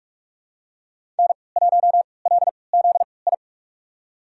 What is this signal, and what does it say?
Morse code, heard as a keyed tone, about 46 wpm, spaced as for 25 wpm, 700 Hz. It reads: N1LZI
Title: USA Call Signs - 3x Character Spacing - 45wpm